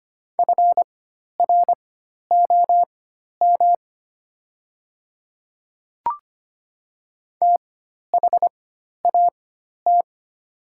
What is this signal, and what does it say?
Morse code, heard as a keyed tone, about 25 wpm, spaced as for 12 wpm, 700 Hz. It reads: FROM THAT
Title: 4x Character Spacing - Top 100 Words - 25wpm